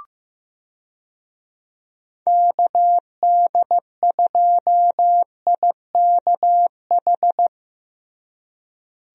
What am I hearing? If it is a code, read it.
Morse code, heard as a keyed tone, about 15 wpm, 700 Hz. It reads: KD2IKH
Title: USA Call Signs - 6 Characters - Repeated 2x - 15wpm